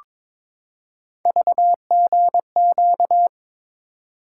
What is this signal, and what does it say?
Morse code, heard as a keyed tone, about 22 wpm, 700 Hz. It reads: VGQ
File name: practice / 22wpm / Three Letter-Number Combinations - 22wpm